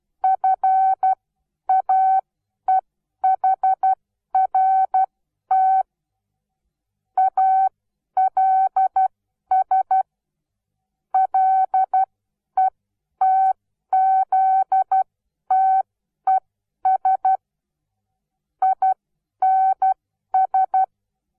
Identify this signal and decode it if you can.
Morse code, heard as a keyed tone, about 12 wpm, spaced as for 10 wpm, 760 Hz. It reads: FAEHRT ALS LETZTES INS
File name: Onlinelehrgang / morsen / DJ4UF Morsen17